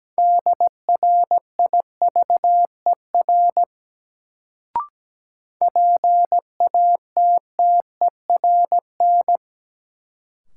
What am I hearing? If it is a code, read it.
Morse code, heard as a keyed tone, about 17 wpm, 700 Hz. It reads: DRIVER PATTERN
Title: Sentences from Top 1000 Words - New Word Drill - 17wpm